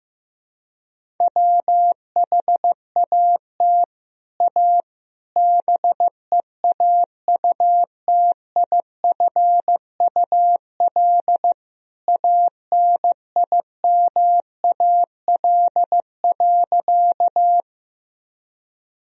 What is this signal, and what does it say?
Morse code, heard as a keyed tone, about 15 wpm, 700 Hz. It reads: WHAT A BEAUTIFUL ANIMAL.
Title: Sentences from Top 1000 Words - Review Part 1 - 15wpm